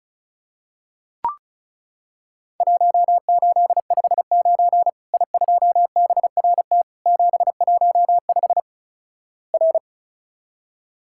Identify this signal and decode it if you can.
Morse code, heard as a keyed tone, about 36 wpm, 700 Hz. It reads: 1859 I2BRT 715 R
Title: Worked All Europe DX Contest - QTC Exchange - 35wpm